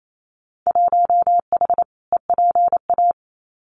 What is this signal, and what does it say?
Morse code, heard as a keyed tone, about 28 wpm, 700 Hz. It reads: 1H EPA
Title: Winter Field Day Exchanges 28wpm